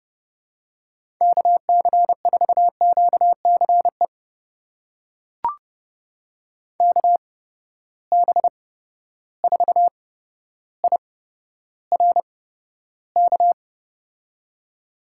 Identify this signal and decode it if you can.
Morse code, heard as a keyed tone, about 30 wpm, spaced as for 13 wpm, 700 Hz. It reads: KC4QCE KB4IRK
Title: USA Call Signs - 6 Characters - 8x Character Spacing - 30wpm